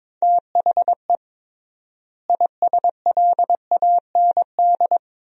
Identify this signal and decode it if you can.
Morse code, heard as a keyed tone, about 22 wpm, 700 Hz. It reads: THE ISLAND
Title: 3x Word-Spacing - Sentences from Top 500 Words - 22wpm